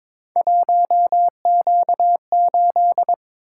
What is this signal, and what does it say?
Morse code, heard as a keyed tone, about 23 wpm, 700 Hz. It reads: 1Q8
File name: practice / 22wpm / Three Letter-Number Combinations - 22wpm